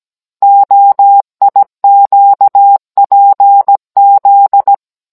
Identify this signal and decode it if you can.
Morse code, heard as a keyed tone, about 17 wpm, 800 Hz. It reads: OIQPZ
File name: Cours-NR18-16mm